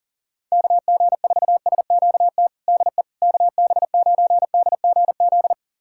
Morse code, heard as a keyed tone, about 40 wpm, 700 Hz. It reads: KG4SQT DE KB9DGZ
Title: USA Callsign DE Callsign - 6 Characters - 40wpm